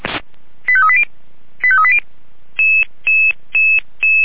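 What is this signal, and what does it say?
Five-tone selective call (ZVEI), heard as quick runs of five tones, tones of about 70 ms. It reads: 85380 85380
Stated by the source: ZVEI five-tone selective call 89155 received by SDR, logged on the Feuerwehr Orlamünde fire brigade server.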